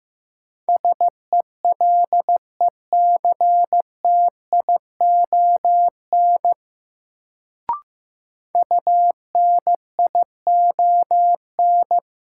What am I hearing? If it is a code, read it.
Morse code, heard as a keyed tone, about 15 wpm, 700 Hz. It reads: SELECTION UNION
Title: Sentences from Top 1400 Words - New Word Drill - 15wpm